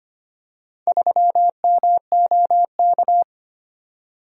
Morse code, heard as a keyed tone, about 25 wpm, 700 Hz. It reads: 3MOK
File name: Four Letter-Number Combinations - 25wpm